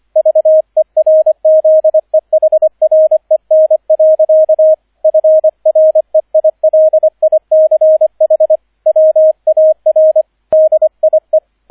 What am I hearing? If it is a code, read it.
Morse code, heard as a keyed tone, about 25 wpm, 630 Hz, with radio noise. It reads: VERZEHREN. FREILICH WAR DIE